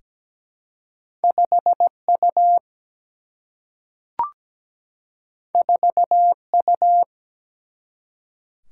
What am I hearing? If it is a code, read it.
Morse code, heard as a keyed tone, about 17 wpm, 700 Hz. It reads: 5U 4U
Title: Single-Letter Single-Number 17wpm